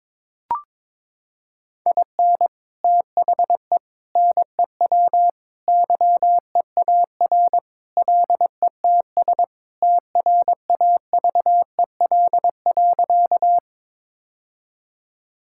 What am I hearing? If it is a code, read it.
Morse code, heard as a keyed tone, about 22 wpm, 700 Hz. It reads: IN THE NEW YEAR LETS TRAVEL.
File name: practice / 22wpm / Sentences with 3-Word Phrases - 22wpm